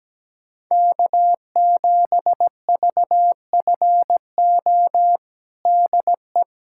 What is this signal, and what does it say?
Morse code, heard as a keyed tone, about 17 wpm, 700 Hz. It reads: K7VFO DE
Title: USA Callsign DE Callsign - 5 Characters - 17wpm